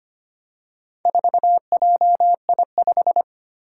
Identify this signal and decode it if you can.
Morse code, heard as a keyed tone, about 25 wpm, 700 Hz. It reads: 4JI5
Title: Four Challenging Letter-Number Combinations 25wpm